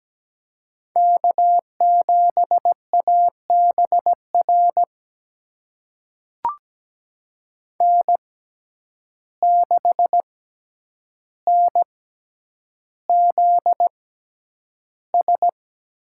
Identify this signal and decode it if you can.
Morse code, heard as a keyed tone, about 17 wpm, spaced as for 9 wpm, 700 Hz. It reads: K7ABR N6NZS
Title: USA Call Signs - 5 Characters - 6x Character Spacing - 17wpm